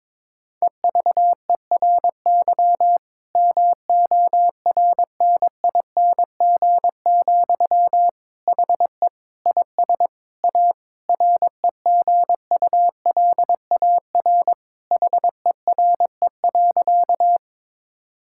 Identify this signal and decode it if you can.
Morse code, heard as a keyed tone, about 22 wpm, 700 Hz. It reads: EVERY MORNING, HE IS A REGULAR HERE.
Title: Sentences from Top 1000 Words - Review Part 6 - 22wpm